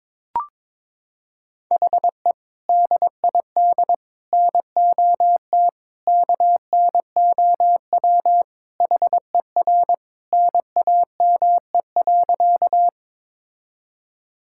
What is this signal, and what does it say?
Morse code, heard as a keyed tone, about 22 wpm, 700 Hz. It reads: HE DID NOT KNOW HER NAME.
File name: practice / 22wpm / Sentences with 4-Word Phrases - 22wpm